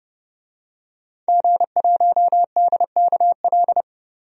Morse code, heard as a keyed tone, about 30 wpm, 700 Hz. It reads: G1DKL
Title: DXCC-100 Call Signs - 5 Characters - 30wpm